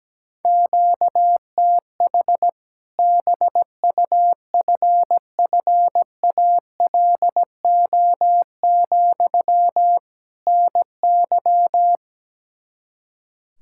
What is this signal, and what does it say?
Morse code, heard as a keyed tone, about 17 wpm, 700 Hz. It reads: QTH BUFFALO, NY
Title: Practice QSO - Exchange 1 - Beginning - 17wpm